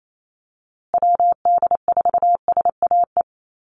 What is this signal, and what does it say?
Morse code, heard as a keyed tone, about 28 wpm, 700 Hz. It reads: WD4SAE
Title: USA Call Signs Repeated 3x 28wpm